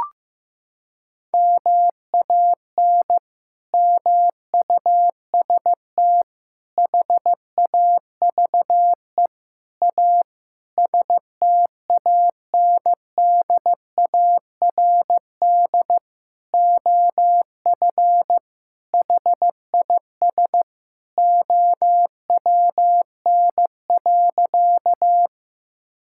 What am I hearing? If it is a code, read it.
Morse code, heard as a keyed tone, about 15 wpm, 700 Hz. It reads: MAN MUST HAVE A STANDARD OF HIS OWN.
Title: Sentences from Top 1000 Words - Review Part 6 - 15wpm